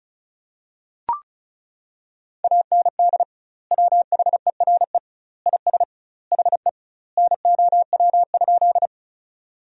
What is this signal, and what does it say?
Morse code, heard as a keyed tone, about 34 wpm, 700 Hz. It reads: AND WHERE IS HE NOW?
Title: 2x Word-Spacing - Sentences from Top 200 Words - 35wpm